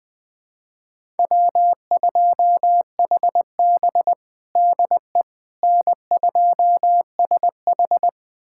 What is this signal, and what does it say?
Morse code, heard as a keyed tone, about 20 wpm, 700 Hz. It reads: W2HB DE N2SH
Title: USA Callsign DE Callsign - 4 Characters - 20wpm